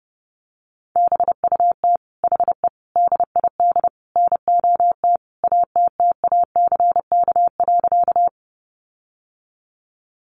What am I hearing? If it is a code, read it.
Morse code, heard as a keyed tone, about 30 wpm, 700 Hz. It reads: BUT HE DID NOT ATTACK.
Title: Sentences from Top 1500 Words 30wpm